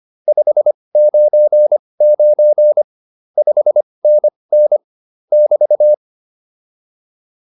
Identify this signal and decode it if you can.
Morse code, heard as a keyed tone, about 25 wpm, spaced as for 20 wpm, 600 Hz. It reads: 599 5NN <BT>